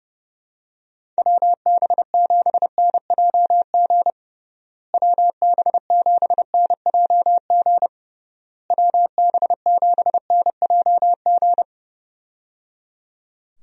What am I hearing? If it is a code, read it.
Morse code, heard as a keyed tone, about 30 wpm, 700 Hz. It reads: WB7NJG WB7NJG WB7NJG
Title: USA Call Signs - 6 Characters - Repeated 3x - 30wpm